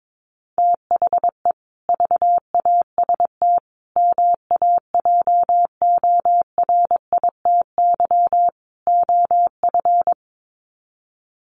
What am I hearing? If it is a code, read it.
Morse code, heard as a keyed tone, about 22 wpm, 700 Hz. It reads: THE VAST MAJORITY OF
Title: Sets of 4 Words 22wpm